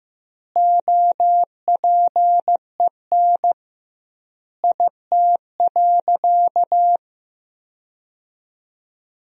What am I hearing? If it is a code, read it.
Morse code, heard as a keyed tone, about 16 wpm, 700 Hz. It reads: OPEN IT.
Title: 2x Word-Spacing - Sentences from Top 300 Words - 15wpm